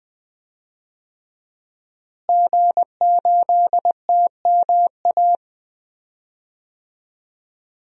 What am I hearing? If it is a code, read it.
Morse code, heard as a keyed tone, about 20 wpm, 700 Hz. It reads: G8TMA
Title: DXCC-100 Call Signs - 5 Characters - Repeated 2x - 20wpm